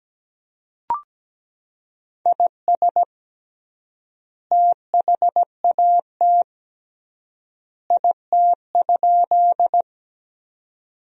Morse code, heard as a keyed tone, about 17 wpm, 700 Hz. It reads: IS THAT IT?
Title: 3x Word-Spacing - Sentences from Top 100 Words - 17wpm